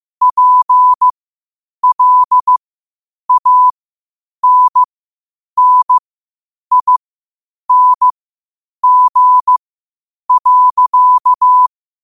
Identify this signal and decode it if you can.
Morse code, heard as a keyed tone, about 15 wpm, spaced as for 8 wpm, 1000 Hz. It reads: PLANNING.
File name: qotdfiles / 8wpm